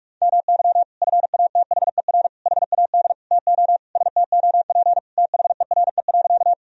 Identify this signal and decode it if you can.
Morse code, heard as a keyed tone, about 41 wpm, 700 Hz. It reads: MY FATHER HAD TO STOP THERE.
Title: Sentences from Top 1000 Words - Review Part 1 - 45wpm